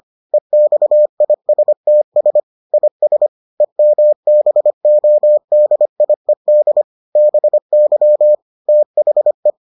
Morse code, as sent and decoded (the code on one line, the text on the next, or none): . -..- .. ... - ... / .. ... / . -- -... --- -.. .. -.. / -... -.-- / - .... .
EXISTS IS EMBODIED BY THE